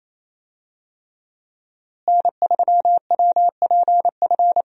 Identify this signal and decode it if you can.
Morse code, heard as a keyed tone, about 28 wpm, 700 Hz. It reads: N3WPF
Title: USA Call Signs - 5 Characters - Repeated 3x - 28wpm